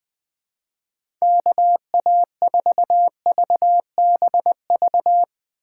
Morse code, heard as a keyed tone, about 21 wpm, 700 Hz. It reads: KA4VBV
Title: USA Call Signs - 2x Character Spacing - 20wpm